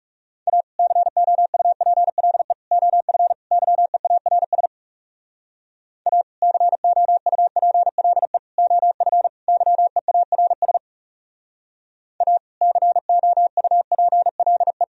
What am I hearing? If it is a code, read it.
Morse code, heard as a keyed tone, about 37 wpm, 700 Hz. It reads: A COUPLE OF YEARS A COUPLE OF YEARS A COUPLE
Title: Speed-Racing - Sets of 4 Words 30wpm